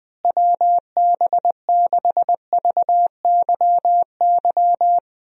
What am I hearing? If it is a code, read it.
Morse code, heard as a keyed tone, about 21 wpm, 700 Hz. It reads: WB6VYY
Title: USA Call Signs - 6 Characters - Repeated 2x - 20wpm